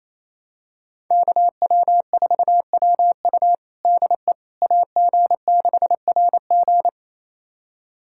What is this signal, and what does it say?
Morse code, heard as a keyed tone, about 28 wpm, 700 Hz. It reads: KW4WU DE AG6RG
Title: USA Callsign DE Callsign - 5 Characters - 28wpm